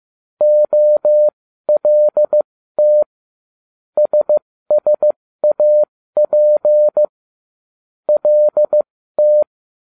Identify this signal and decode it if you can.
Morse code, heard as a keyed tone, about 15 wpm, spaced as for 13 wpm, 600 Hz. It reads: OLT SSAP LT
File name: cw / ubacw12 1